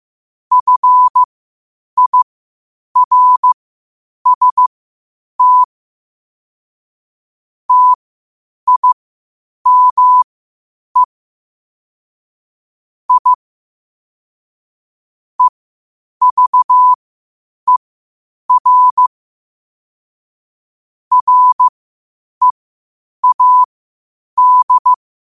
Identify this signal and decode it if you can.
Morse code, heard as a keyed tone, about 15 wpm, spaced as for 8 wpm, 1000 Hz. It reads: FIRST TIME I EVER READ
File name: qotdfiles / 8wpm